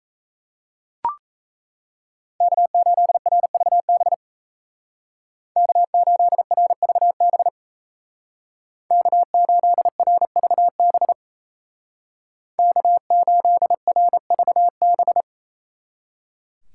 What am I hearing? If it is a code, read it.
Morse code, heard as a keyed tone, about 34 wpm, 700 Hz. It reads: K8RVB K8RVB K8RVB K8RVB
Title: Speed Racing - USA 1x3 Call Signs 28wpm